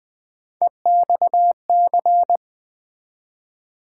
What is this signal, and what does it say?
Morse code, heard as a keyed tone, about 20 wpm, 700 Hz. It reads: EXC
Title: Build A Word - 3 Letters At A Time - 6-Letter Words - 20wpm